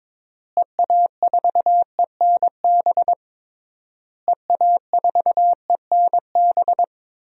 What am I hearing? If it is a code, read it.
Morse code, heard as a keyed tone, about 22 wpm, 700 Hz. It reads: EA4ENB EA4ENB
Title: DXCC-100 Call Signs - 6 Characters - Repeated 3x - 22wpm